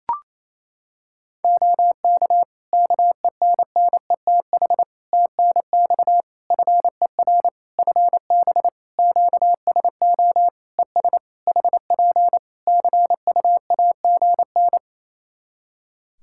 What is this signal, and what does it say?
Morse code, heard as a keyed tone, about 27 wpm, 700 Hz. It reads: OK KENNETH TNX FER FB QSO ES HP CUAGN
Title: Practice QSO - Exchange 4 - Ending - 28wpm